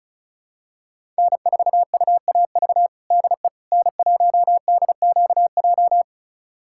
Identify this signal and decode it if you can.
Morse code, heard as a keyed tone, about 34 wpm, 700 Hz. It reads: N4UAV DE N1DQJ
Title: USA Callsign DE Callsign - 5 Characters - 35wpm